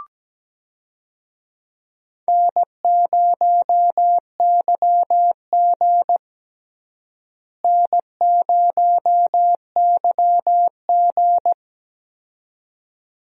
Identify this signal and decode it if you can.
Morse code, heard as a keyed tone, about 17 wpm, 700 Hz. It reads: N0YG N0YG
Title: USA Call Signs - 4 Characters - Repeated 2x - 17wpm